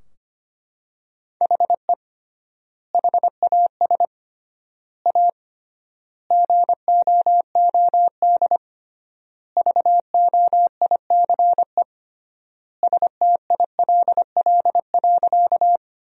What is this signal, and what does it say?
Morse code, heard as a keyed tone, about 25 wpm, 700 Hz. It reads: HE HAS A GOOD VOICE STILL.
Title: 3x Word-Spacing - Sentences from Top 500 Words - 25wpm